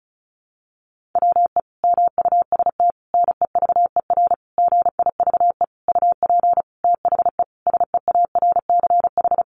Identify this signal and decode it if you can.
Morse code, heard as a keyed tone, about 34 wpm, 700 Hz. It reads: WE MUST NEVER GIVE UP THE SEARCH
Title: Sentences from Top 1500 Words 35wpm